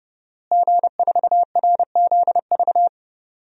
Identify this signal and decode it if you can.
Morse code, heard as a keyed tone, about 30 wpm, 700 Hz. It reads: G4RZV
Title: DXCC-100 Call Signs - 5 Characters - Repeated 3x - 30wpm